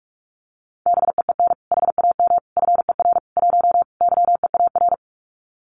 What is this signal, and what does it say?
Morse code, heard as a keyed tone, about 46 wpm, 700 Hz. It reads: BEEN HAM FER 1 YEAR
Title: Common QSO Phrases Encore 45wpm